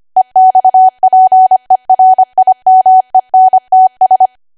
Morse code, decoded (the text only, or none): EXPERIMENTS